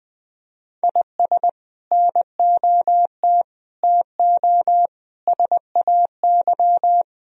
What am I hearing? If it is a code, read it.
Morse code, heard as a keyed tone, about 20 wpm, 700 Hz. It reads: IS NOT TO SAY
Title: Speed-Racing - Sets of 5 Words 20wpm